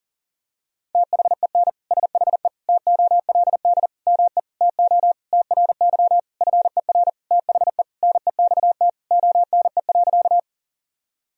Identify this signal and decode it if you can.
Morse code, heard as a keyed tone, about 40 wpm, 700 Hz. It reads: THEN SHE TOLD ME TO TRY FER THE NEXT ONE.
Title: Sentences from Top 400 Words and Abbreviations 40wpm